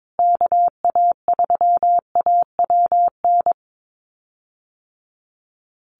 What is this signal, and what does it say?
Morse code, heard as a keyed tone, about 22 wpm, 700 Hz. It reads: KA3AWN
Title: USA Call Signs Repeated 2x 22wpm